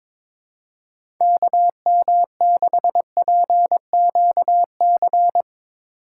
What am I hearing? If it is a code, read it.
Morse code, heard as a keyed tone, about 21 wpm, 700 Hz. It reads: KM6PQC
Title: USA Call Signs - 6 Characters - 2x Character Spacing - 22wpm